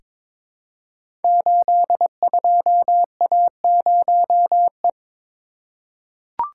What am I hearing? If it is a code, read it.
Morse code, heard as a keyed tone, about 22 wpm, 690 Hz. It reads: 82A0E
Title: Code Groups - 5 Letter:Numbers - 22wpm